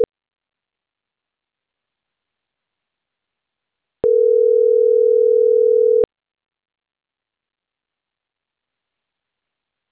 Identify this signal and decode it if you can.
Ringback tone (North America), 440 + 480 Hz.